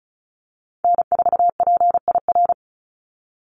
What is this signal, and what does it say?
Morse code, heard as a keyed tone, about 34 wpm, 700 Hz. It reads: N4PIR
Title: Call Signs Repeated 3x - 35wpm